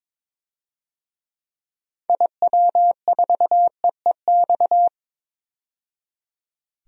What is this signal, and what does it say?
Morse code, heard as a keyed tone, about 22 wpm, 700 Hz. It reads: IW4EEX
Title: DXCC-100 Call Signs - 6 Characters - 22wpm